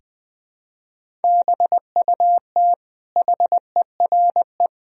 Morse code, heard as a keyed tone, about 21 wpm, 700 Hz. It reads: BUT HERE
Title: Sentences from Top 1000 Words - Review Part 1 - 20wpm